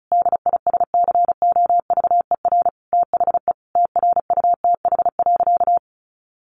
Morse code, heard as a keyed tone, about 34 wpm, 700 Hz. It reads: DISCOVER THE TRUTH.
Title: Sentences from Top 1100 Words 35wpm